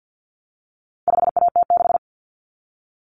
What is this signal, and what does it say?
Morse code, heard as a keyed tone, about 50 wpm, 700 Hz. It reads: 5AT6
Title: Common Cut Numbers - 4-Digit 50wpm